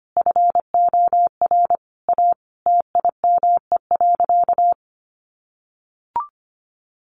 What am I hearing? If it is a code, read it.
Morse code, heard as a keyed tone, about 25 wpm, 700 Hz. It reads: FOR A TIME.
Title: Sentences from Top 700 Words 25wpm